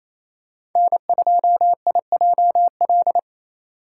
Morse code, heard as a keyed tone, about 28 wpm, 700 Hz. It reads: N2IJL
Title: USA Call Signs - 5 Characters - 8x Character Spacing - 28wpm